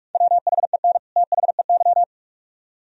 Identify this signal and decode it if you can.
Morse code, heard as a keyed tone, about 59 wpm, spaced as for 52 wpm, 700 Hz. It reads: WHEN THEY
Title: Speed-Racing - Sets of 2 Words 45wpm